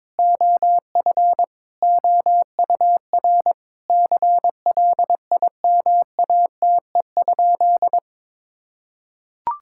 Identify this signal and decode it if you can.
Morse code, heard as a keyed tone, about 22 wpm, 700 Hz. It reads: OF OUR CLIMATE?
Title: Sentences from Top 1600 Words 22wpm